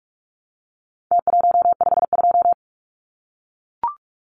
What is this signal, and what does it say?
Morse code, heard as a keyed tone, about 44 wpm, 700 Hz. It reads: T152
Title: Numbers - Contest Style - 4-Digit 45wpm